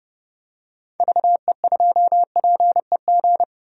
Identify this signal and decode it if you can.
Morse code, heard as a keyed tone, about 30 wpm, 700 Hz. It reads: VE2PEG
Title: DXCC-100 Call Signs - 6 Characters - 30wpm